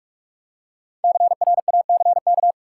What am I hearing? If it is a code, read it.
Morse code, heard as a keyed tone, about 46 wpm, 700 Hz. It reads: CRACK
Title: Five Letter Words - 45wpm